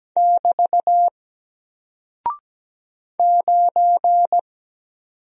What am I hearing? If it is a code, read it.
Morse code, heard as a keyed tone, about 17 wpm, 700 Hz. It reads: <BT> 9
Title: Single Character (Letter:Number:Punctuation:Prosign) Alt 17wpm